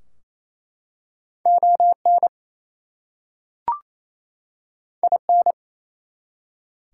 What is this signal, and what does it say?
Morse code, heard as a keyed tone, about 27 wpm, 700 Hz. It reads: ON IN
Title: Two Letter Words - 28wpm